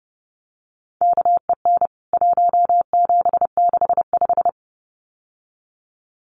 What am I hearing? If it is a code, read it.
Morse code, heard as a keyed tone, about 30 wpm, 700 Hz. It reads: KEN 1765
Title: CWT Exchange 30wpm